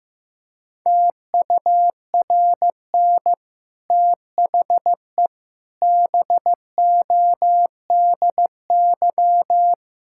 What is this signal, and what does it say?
Morse code, heard as a keyed tone, about 15 wpm, 700 Hz. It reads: TURN THE BODY
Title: Sentences from Top 1000 Words - Review Part 6 - 15wpm